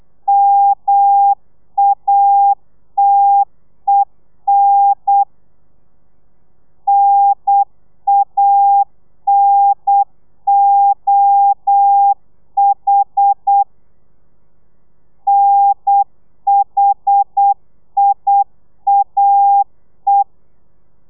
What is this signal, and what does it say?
Morse code, heard as a keyed tone, about 8 wpm, 800 Hz. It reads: MATEN NANOH NHIAE